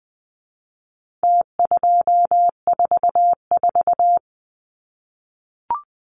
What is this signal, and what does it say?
Morse code, heard as a keyed tone, about 20 wpm, 700 Hz. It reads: T244
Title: Numbers - Contest Style - 4-Digit 20wpm